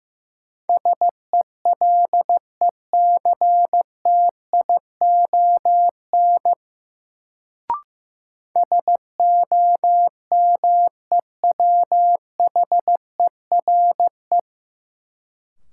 Morse code, heard as a keyed tone, about 15 wpm, 700 Hz. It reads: SELECTION SOMEWHERE
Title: Sentences from Top 1400 Words - New Word Drill - 15wpm